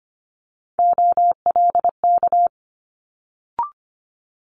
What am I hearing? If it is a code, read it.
Morse code, heard as a keyed tone, about 25 wpm, 700 Hz. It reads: OLK
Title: Multi-letter Phonemes Quick Fire 25wpm